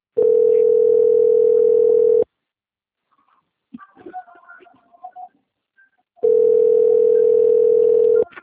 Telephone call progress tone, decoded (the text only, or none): ringback tone